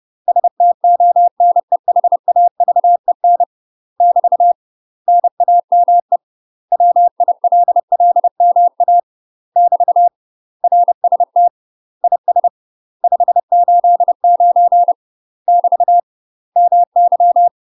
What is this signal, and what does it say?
Morse code, heard as a keyed tone, about 30 wpm, 700 Hz. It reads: STONEHAVEN <BT> NAME WILLMA <BT> RST IS 589 <BT> MY
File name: UK MASTER QSOs TRAINING / 30WPM / UK MASTER QSOs 0041